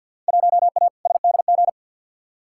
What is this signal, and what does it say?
Morse code, heard as a keyed tone, about 46 wpm, 700 Hz. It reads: 1A SDG
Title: Field Day Exchanges Encore - 50wpm